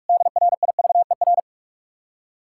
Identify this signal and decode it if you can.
Morse code, heard as a keyed tone, about 58 wpm, spaced as for 51 wpm, 700 Hz. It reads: DRIVER
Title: Speed-Racing Common QSO Phrases - 45wpm